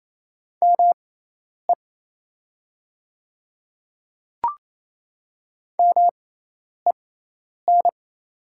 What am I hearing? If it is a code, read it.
Morse code, heard as a keyed tone, about 28 wpm, spaced as for 10 wpm, 700 Hz. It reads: ME MEN
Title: 6x Character Spacing - Top 500 Words - 28wpm